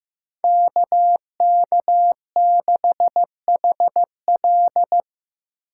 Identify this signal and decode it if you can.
Morse code, heard as a keyed tone, about 18 wpm, 700 Hz. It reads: KK6HL
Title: Speed Racing - USA 2x2 Call Signs 15wpm